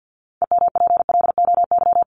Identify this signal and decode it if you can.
Morse code, heard as a keyed tone, about 50 wpm, 700 Hz. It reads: EMPLOY
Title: ED Suffix - Top 100 Words with Base Words 50wpm